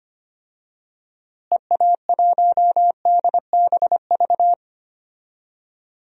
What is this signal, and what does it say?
Morse code, heard as a keyed tone, about 25 wpm, 700 Hz. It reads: EA1DBV
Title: DXCC-100 Call Signs - 6 Characters - Repeated 3x - 25wpm